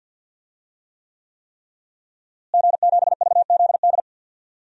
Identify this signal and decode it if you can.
Morse code, heard as a keyed tone, about 47 wpm, 700 Hz. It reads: G7VZD
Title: DXCC-100 Call Signs - 5 Characters - Repeated 3x - 50wpm